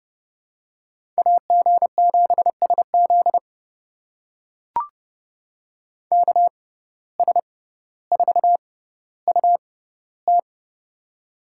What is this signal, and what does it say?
Morse code, heard as a keyed tone, about 30 wpm, spaced as for 15 wpm, 700 Hz. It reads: AG7SZ KS4UT
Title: USA Call Signs - 5 Characters - 6x Character Spacing - 30wpm